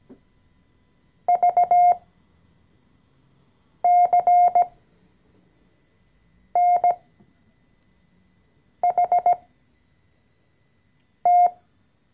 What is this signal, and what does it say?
Morse code, heard as a keyed tone, about 17 wpm, spaced as for 4 wpm, 710 Hz. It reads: VCNHT
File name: CW-Online-Kurs / Anfänger / Lektion 014